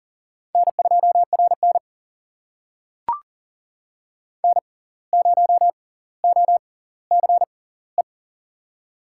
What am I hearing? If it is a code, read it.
Morse code, heard as a keyed tone, about 38 wpm, spaced as for 19 wpm, 700 Hz. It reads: N2RN N0OCE
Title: USA Call Signs - 6x Character Spacing - 40wpm